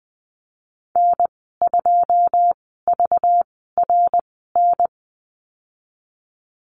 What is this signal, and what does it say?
Morse code, heard as a keyed tone, about 20 wpm, spaced as for 14 wpm, 700 Hz. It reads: N2VRN
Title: 231 Lesson 37g - Letter x - Call Signs 20wpm